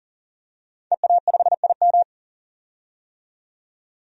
Morse code, heard as a keyed tone, about 40 wpm, 700 Hz. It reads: EA5IM
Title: DXCC-100 Call Signs - 5 Characters - 40wpm